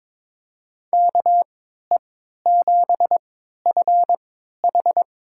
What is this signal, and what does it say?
Morse code, heard as a keyed tone, about 22 wpm, spaced as for 13 wpm, 700 Hz. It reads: KE7FH
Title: USA Call Signs - 3x Character Spacing - 22wpm